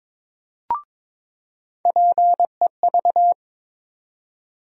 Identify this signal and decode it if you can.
Morse code, heard as a keyed tone, about 21 wpm, 700 Hz. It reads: PEV
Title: USA Call Sign Postfixes - 3 Letters 22wpm